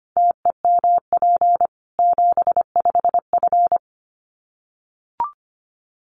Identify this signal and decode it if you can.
Morse code, heard as a keyed tone, about 25 wpm, 700 Hz. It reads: TEMP 75F
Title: Common QSO Phrases 25wpm